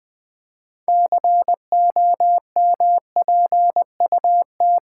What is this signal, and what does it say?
Morse code, heard as a keyed tone, about 20 wpm, 700 Hz. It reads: COMPUT
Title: Build A Word - 3 Letters At A Time - 8-Letter Words - 20wpm